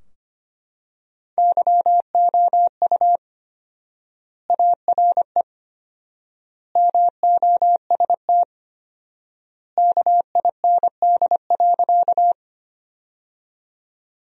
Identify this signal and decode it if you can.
Morse code, heard as a keyed tone, about 25 wpm, 700 Hz. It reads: YOU ARE MOST KIND.
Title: 4x Word-Spacing - Sentences from Top 200 Words - 25wpm